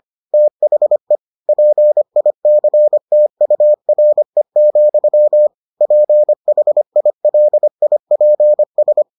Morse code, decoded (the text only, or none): THE PICTURE, PHILIPS